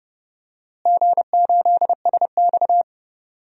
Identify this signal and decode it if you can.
Morse code, heard as a keyed tone, about 30 wpm, 700 Hz. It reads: G8SX
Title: DXCC-100 Call Signs - 4 Characters - Repeated 2x - 30wpm